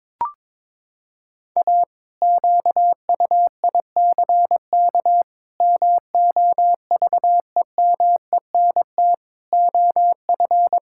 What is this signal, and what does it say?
Morse code, heard as a keyed tone, about 22 wpm, 700 Hz. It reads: A QUICK MOVEMENT OF
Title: Pangram Challenge v2 - 22wpm